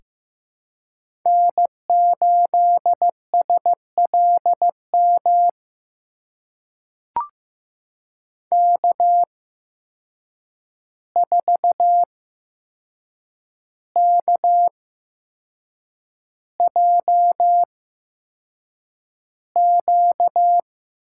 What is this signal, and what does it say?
Morse code, heard as a keyed tone, about 15 wpm, spaced as for 6 wpm, 700 Hz. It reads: N8SLM K4KJQ
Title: USA Call Signs - 5 Characters - 8x Character Spacing - 15wpm